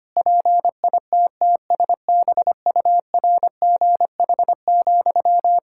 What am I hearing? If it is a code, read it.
Morse code, heard as a keyed tone, about 25 wpm, 700 Hz. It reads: PITTSBURGH,
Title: Practice QSO - Exchange 1 - Beginning - 25wpm